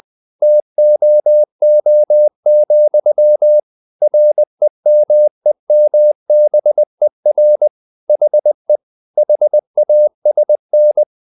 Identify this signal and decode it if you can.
Morse code, heard as a keyed tone, about 20 wpm, 600 Hz. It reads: TOO, REMEMBER HE HASN
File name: the old man and the sea part 2 20wpm